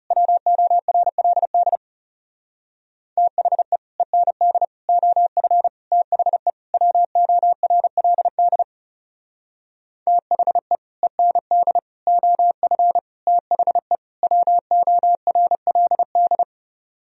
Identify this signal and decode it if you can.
Morse code, heard as a keyed tone, about 36 wpm, 700 Hz. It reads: WORLD THE END OF THE WORLD THE END OF THE WORLD
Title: Speed-Racing - Sets of 5 Words 30wpm